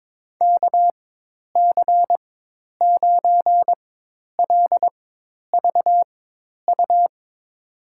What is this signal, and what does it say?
Morse code, heard as a keyed tone, about 22 wpm, spaced as for 10 wpm, 700 Hz. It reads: KC9LVU
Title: USA Call Signs - 4x Character Spacing - 22wpm